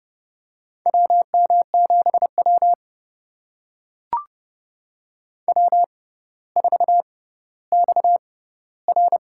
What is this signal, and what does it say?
Morse code, heard as a keyed tone, about 30 wpm, spaced as for 15 wpm, 700 Hz. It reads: WM7W W4XR
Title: USA Call Signs - 4 Characters - 6x Character Spacing - 30wpm